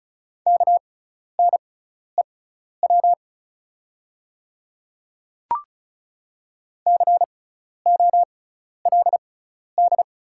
Morse code, heard as a keyed tone, about 34 wpm, spaced as for 12 wpm, 690 Hz. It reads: KNEW COLD
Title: 6x Character Spacing - Top 500 Words - 35wpm